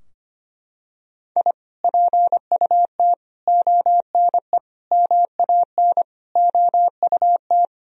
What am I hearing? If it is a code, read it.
Morse code, heard as a keyed tone, about 25 wpm, 700 Hz. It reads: I PUT ONE MAN OUT
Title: Sentences from Top 1000 Words - Review Part 2 - 25wpm